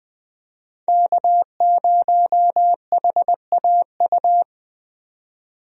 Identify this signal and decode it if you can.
Morse code, heard as a keyed tone, about 20 wpm, 700 Hz. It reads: K0HAU
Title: USA Call Signs - 5 Characters - 20wpm